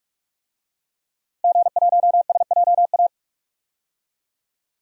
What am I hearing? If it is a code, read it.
Morse code, heard as a keyed tone, about 45 wpm, 700 Hz. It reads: G1SJA
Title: DXCC-100 Call Signs - 5 Characters - Repeated 2x - 45wpm